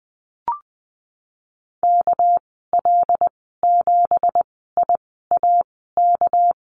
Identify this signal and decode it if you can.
Morse code, heard as a keyed tone, about 19 wpm, spaced as for 14 wpm, 700 Hz. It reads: KL7IAK
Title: 200 Lesson 32g - Letter k - Call Signs 20wpm